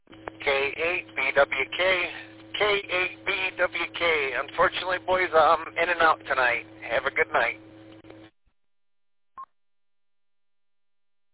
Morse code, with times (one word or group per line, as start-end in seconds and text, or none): none